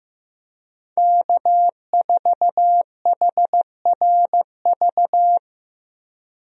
Morse code, heard as a keyed tone, about 15 wpm, 700 Hz. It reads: K4HRV
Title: USA Call Signs - 5 Characters - Repeated 2x - 15wpm